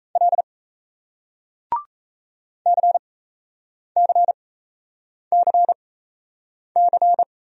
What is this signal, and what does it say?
Morse code, heard as a keyed tone, about 38 wpm, spaced as for 8 wpm, 700 Hz. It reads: L CCCC